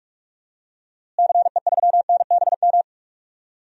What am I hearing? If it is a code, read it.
Morse code, heard as a keyed tone, about 43 wpm, 700 Hz. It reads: CE3NBM
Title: DXCC-100 Call Signs - 6 Characters - Repeated 3x - 45wpm